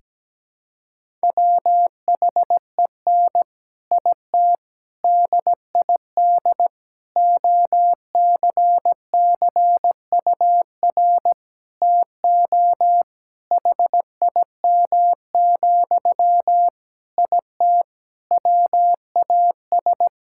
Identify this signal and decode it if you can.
Morse code, heard as a keyed tone, about 17 wpm, 700 Hz. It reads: WHEN IT DID OCCUR TO HIM, IT WAS
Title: Sentences from Top 1000 Words - Review Part 6 - 17wpm